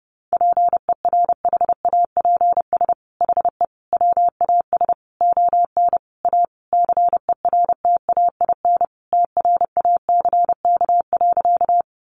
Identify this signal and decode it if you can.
Morse code, heard as a keyed tone, about 30 wpm, 700 Hz. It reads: PERHAPS HE WAS ON A CERTAIN TRACK.
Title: Sentences from Top 1100 Words 30wpm